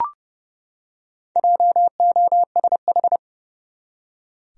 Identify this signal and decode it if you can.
Morse code, heard as a keyed tone, about 31 wpm, 700 Hz. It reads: JOSH
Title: USA Common Names - Max 5 Letters - 30wpm